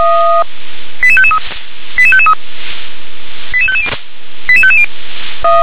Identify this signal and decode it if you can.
Five-tone selective call (ZVEI), heard as quick runs of five tones, tones of about 70 ms. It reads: 88553 88553 88550 88550